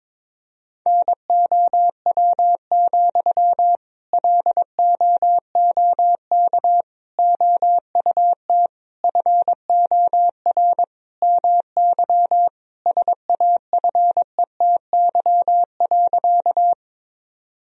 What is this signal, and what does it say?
Morse code, heard as a keyed tone, about 22 wpm, 700 Hz. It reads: NOW, LOOK OUT FOR MY SAFETY.
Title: Sentences from Top 1000 Words - Review Part 4 - 22wpm